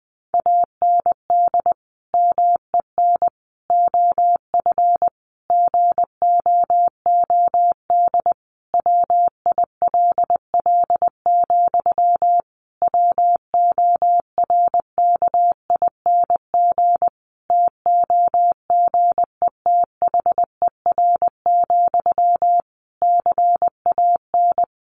Morse code, heard as a keyed tone, about 20 wpm, 700 Hz. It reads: AND MEN OF GOOD WILL, WORKING TOGETHER, CAN
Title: Sentences from Top 1400 Words 20wpm